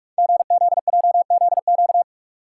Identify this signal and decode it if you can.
Morse code, heard as a keyed tone, about 44 wpm, 700 Hz. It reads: GZJZQ
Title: Code Groups - 5 Letter:Numbers - 45wpm